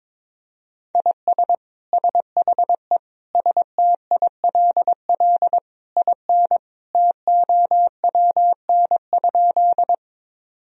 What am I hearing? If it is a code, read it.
Morse code, heard as a keyed tone, about 22 wpm, 700 Hz. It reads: IS SHE STILL IN TOWN?
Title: Sentences from Top 1000 Words - Review Part 2 - 22wpm